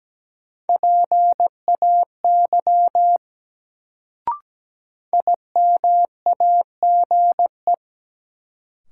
Morse code, heard as a keyed tone, about 17 wpm, 700 Hz. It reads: PAY IMAGE
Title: Sentences from Top 700 Words - New Word Drill - 17wpm